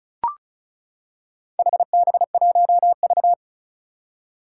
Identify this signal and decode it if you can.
Morse code, heard as a keyed tone, about 36 wpm, 700 Hz. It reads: HB1V
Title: Four Challenging Letter-Number Combinations 35wpm